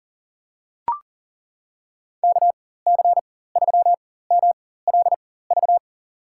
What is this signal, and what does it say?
Morse code, heard as a keyed tone, about 40 wpm, spaced as for 19 wpm, 700 Hz. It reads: KC3MLV
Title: USA Call Signs - 6 Characters - 4x Character Spacing - 40wpm